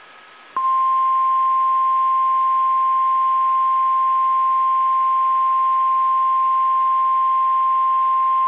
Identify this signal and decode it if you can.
Emergency Alert System (SAME) tones (the 1050 Hz attention tone). It is heard from 0.6 to 8.2 s.
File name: eas / KSDZKDJL / KSDZKDJL